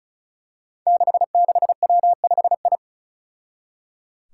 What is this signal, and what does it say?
Morse code, heard as a keyed tone, about 36 wpm, 700 Hz. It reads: 66W5I
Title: Five Challenging Letter-Number Combinations 35wpm